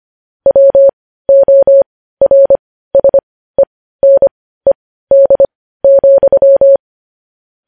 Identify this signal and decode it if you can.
Morse code, heard as a keyed tone, about 25 wpm, spaced as for 15 wpm, 550 Hz. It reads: WORSENED,
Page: News Headlines in Morse Code at 15 WPM